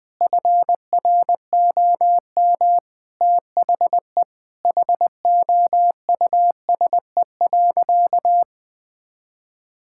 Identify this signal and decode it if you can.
Morse code, heard as a keyed tone, about 21 wpm, 700 Hz. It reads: FROM THE HOUSE.
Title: Sentences from Top 1600 Words 20wpm